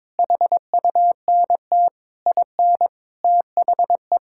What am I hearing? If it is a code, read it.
Morse code, heard as a keyed tone, about 22 wpm, 700 Hz. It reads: HUNT IN THE
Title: Sentences from Top 1600 Words 22wpm